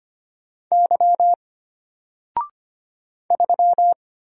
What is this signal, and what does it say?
Morse code, heard as a keyed tone, about 25 wpm, 700 Hz. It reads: Y 3
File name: Single-Letter-Number-25wpm